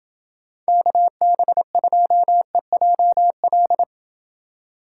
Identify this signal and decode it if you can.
Morse code, heard as a keyed tone, about 29 wpm, 700 Hz. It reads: KB2EJL